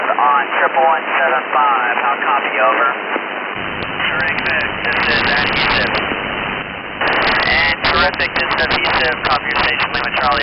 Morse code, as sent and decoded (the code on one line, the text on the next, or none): none